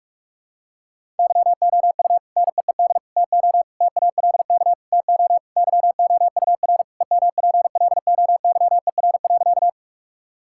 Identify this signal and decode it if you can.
Morse code, heard as a keyed tone, about 46 wpm, 700 Hz. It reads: YOU NEED TO TALK TO YOUR EMPLOYER.
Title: Sentences from Top 1600 Words 45wpm